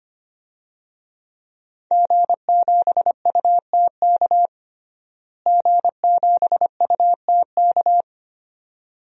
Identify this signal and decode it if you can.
Morse code, heard as a keyed tone, about 25 wpm, 700 Hz. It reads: G7UTK G7UTK
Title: DXCC-100 Call Signs - 5 Characters - Repeated 2x - 25wpm